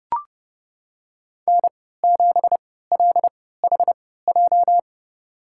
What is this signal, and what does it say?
Morse code, heard as a keyed tone, about 30 wpm, spaced as for 17 wpm, 700 Hz. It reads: N7LHJ
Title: USA Call Signs - 5 Characters - 3x Character Spacing - 30wpm